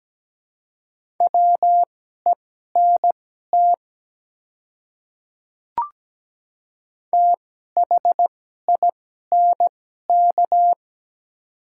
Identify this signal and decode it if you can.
Morse code, heard as a keyed tone, about 17 wpm, spaced as for 12 wpm, 700 Hz. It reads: WENT THINK
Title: 2x Character Spacing - Top 200 Words - 17wpm